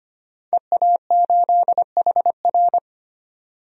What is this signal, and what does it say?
Morse code, heard as a keyed tone, about 25 wpm, 700 Hz. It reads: EA8HR
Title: DXCC-100 Call Signs - 5 Characters - 25wpm